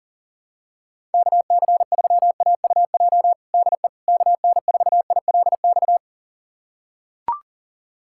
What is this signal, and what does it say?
Morse code, heard as a keyed tone, about 40 wpm, 700 Hz. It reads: KC3AUJ DE KN4ILX
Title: USA Callsign DE Callsign - 6 Characters - 40wpm